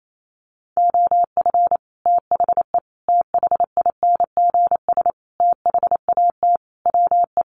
Morse code, heard as a keyed tone, about 28 wpm, 700 Hz. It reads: OF THE THINGS THAT WE
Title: Sets of 5 Words Encore 28wpm